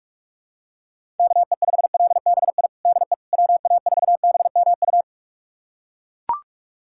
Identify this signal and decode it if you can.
Morse code, heard as a keyed tone, about 42 wpm, 700 Hz. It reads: KE5LBI DE WA4BMU
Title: USA Callsign DE Callsign - 6 Characters - 45wpm